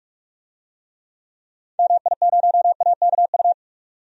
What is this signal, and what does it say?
Morse code, heard as a keyed tone, about 46 wpm, 700 Hz. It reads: MI0AKU
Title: DXCC-100 Call Signs - 6 Characters - 45wpm